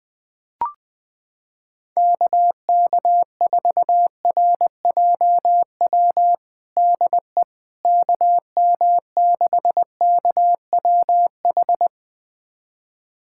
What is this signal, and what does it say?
Morse code, heard as a keyed tone, about 20 wpm, 700 Hz. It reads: KK4RJW DE KM6KWH
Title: USA Callsign DE Callsign 20wpm